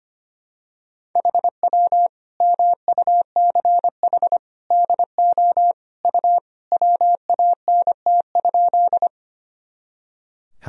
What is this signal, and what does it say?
Morse code, heard as a keyed tone, about 25 wpm, 700 Hz. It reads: HW MUCH DO U WANT?
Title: Sentences from Top 200 Words and Abbreviations 25wpm